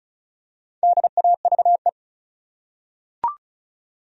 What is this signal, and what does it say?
Morse code, heard as a keyed tone, about 36 wpm, 700 Hz. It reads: DAVE
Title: USA Common Names - Max 5 Letters - 35wpm